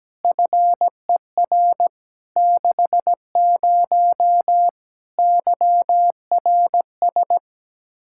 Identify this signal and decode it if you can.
Morse code, heard as a keyed tone, about 17 wpm, 700 Hz. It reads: FER 60 YRS
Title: K1USN SST QSO Practice 17wpm